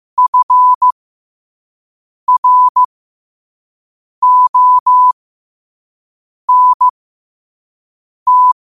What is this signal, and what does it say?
Morse code, heard as a keyed tone, about 15 wpm, spaced as for 5 wpm, 1000 Hz. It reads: FRONT
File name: qotdfiles / 5wpm